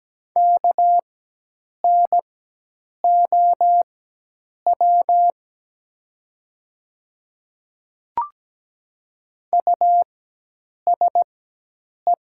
Morse code, heard as a keyed tone, about 17 wpm, spaced as for 8 wpm, 700 Hz. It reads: KNOW USE
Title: 4x Character Spacing - Top 100 Words - 17wpm